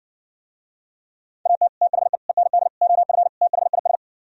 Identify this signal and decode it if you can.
Morse code, heard as a keyed tone, about 59 wpm, 700 Hz. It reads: AT THE END OF THIS